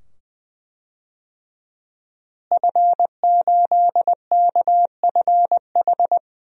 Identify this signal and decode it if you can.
Morse code, heard as a keyed tone, about 21 wpm, 700 Hz. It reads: F8KFH
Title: DXCC-100 Call Signs - 5 Characters - Repeated 2x - 20wpm